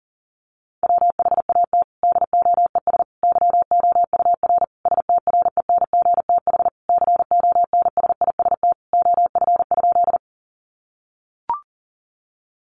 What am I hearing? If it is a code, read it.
Morse code, heard as a keyed tone, about 40 wpm, 700 Hz. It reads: WHAT DOES YOUR STRENGTH CONSIST OF?
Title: Sentences from Top 1400 Words 40wpm